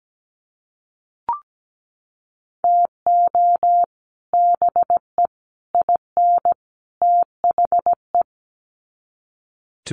Morse code, heard as a keyed tone, about 17 wpm, 700 Hz. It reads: TO BE IN THE
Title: Sets of 4 Words 17wpm